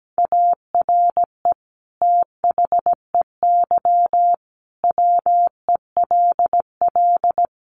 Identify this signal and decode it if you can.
Morse code, heard as a keyed tone, about 17 wpm, 700 Hz. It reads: ARE THEY WELL
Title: Sentences from Top 1300 Words 17wpm